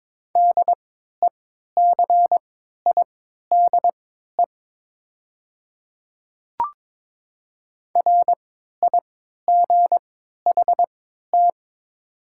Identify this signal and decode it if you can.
Morse code, heard as a keyed tone, about 22 wpm, spaced as for 12 wpm, 700 Hz. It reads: DECIDE RIGHT
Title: 3x Character Spacing - Top 500 Words - 22wpm